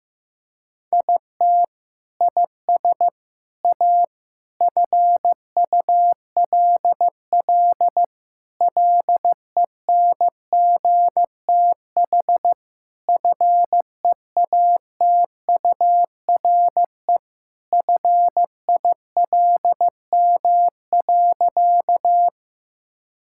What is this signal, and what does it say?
Morse code, heard as a keyed tone, about 15 wpm, 700 Hz. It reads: IT IS A FULL LENGTH FEATURE FILM.
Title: Sentences from Top 1000 Words - Review Part 5 - 15wpm